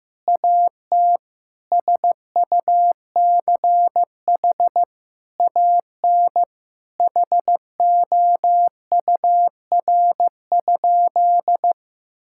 Morse code, heard as a keyed tone, about 15 wpm, 700 Hz. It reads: AT SUCH AN HOUR?
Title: Sentences from Top 1000 Words - Review Part 5 - 15wpm